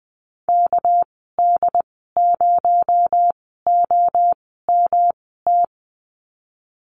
Morse code, heard as a keyed tone, about 20 wpm, spaced as for 14 wpm, 700 Hz. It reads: KD0OMT
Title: 231 Lesson 37g - Letter x - Call Signs 20wpm